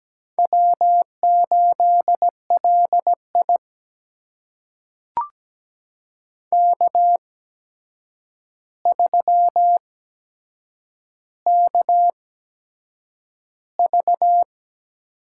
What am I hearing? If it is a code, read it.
Morse code, heard as a keyed tone, about 17 wpm, spaced as for 7 wpm, 700 Hz. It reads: W8LI K3KV
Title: USA Call Signs - 8x Character Spacing - 17wpm